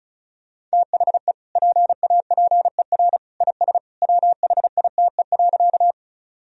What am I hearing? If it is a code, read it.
Morse code, heard as a keyed tone, about 34 wpm, 700 Hz. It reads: THE PAPER IS WHITE.
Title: Sentences from Top 1000 Words - Review Part 1 - 35wpm